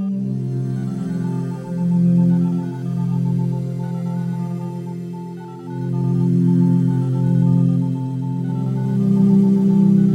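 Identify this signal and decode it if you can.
Morse code, heard as a keyed tone, about 18 wpm, 880 Hz. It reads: 65599114
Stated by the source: Morse code of number codes mixed with binaural waves at 432 Hz.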